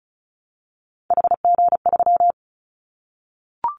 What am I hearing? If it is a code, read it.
Morse code, heard as a keyed tone, about 34 wpm, 700 Hz. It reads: HG3
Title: Callsign Prefixes for Top 25 DXCC - Repeated 3x - 35wpm